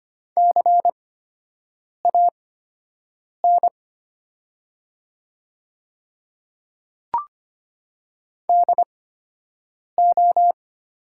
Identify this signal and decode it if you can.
Morse code, heard as a keyed tone, about 25 wpm, spaced as for 7 wpm, 700 Hz. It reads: CAN DO
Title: 8x Character Spacing - Top 100 Words - 25wpm